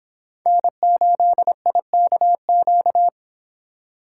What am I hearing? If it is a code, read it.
Morse code, heard as a keyed tone, about 25 wpm, 700 Hz. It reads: N8IKQ